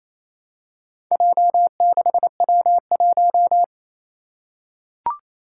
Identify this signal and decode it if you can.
Morse code, heard as a keyed tone, about 29 wpm, 700 Hz. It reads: J6W1
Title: Four Challenging Letter-Number Combinations 28wpm